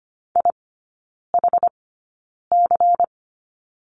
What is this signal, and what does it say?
Morse code, heard as a keyed tone, about 25 wpm, spaced as for 9 wpm, 700 Hz. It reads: IHC
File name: Lesson 17e - Letter u - Three Characters 25wpm